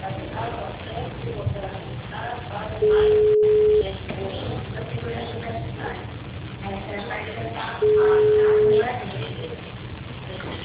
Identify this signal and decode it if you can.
Ringback tone (Europe, 425 Hz), with speech.
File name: out- -2007- - - .44541